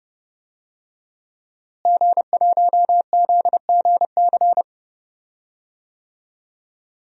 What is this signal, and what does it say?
Morse code, heard as a keyed tone, about 30 wpm, 700 Hz. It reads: G1ZGC
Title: DXCC-100 Call Signs - 5 Characters - Repeated 3x - 30wpm